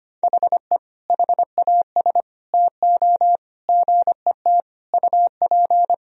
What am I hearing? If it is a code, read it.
Morse code, heard as a keyed tone, about 25 wpm, 700 Hz. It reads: HE HAS TO GET UP
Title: Sentences from Top 1000 Words - Review Part 2 - 25wpm